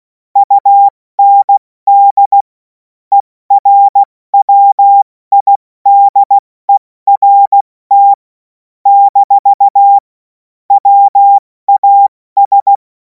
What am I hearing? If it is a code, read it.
Morse code, heard as a keyed tone, about 16 wpm, spaced as for 14 wpm, 800 Hz. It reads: UND ERWIDERT - WAS
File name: Morsekurs / Witze / Witz 03. März 2026 70 F80